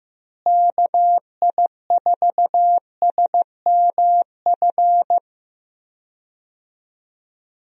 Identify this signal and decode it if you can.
Morse code, heard as a keyed tone, about 15 wpm, 700 Hz. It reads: KI4SMF
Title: USA Call Signs - 6 Characters - Repeated 3x - 15wpm